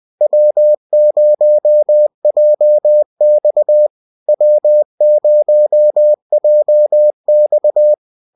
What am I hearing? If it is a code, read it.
Morse code, heard as a keyed tone, about 20 wpm, 600 Hz. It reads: W0JX W0JX